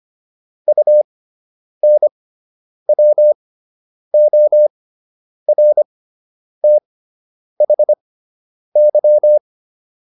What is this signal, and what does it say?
Morse code, heard as a keyed tone, about 25 wpm, spaced as for 9 wpm, 600 Hz. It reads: UNWORTHY